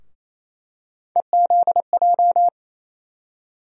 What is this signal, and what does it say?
Morse code, heard as a keyed tone, about 28 wpm, 700 Hz. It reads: EZJ
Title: USA Call Sign Postfixes - 3 Letters 28wpm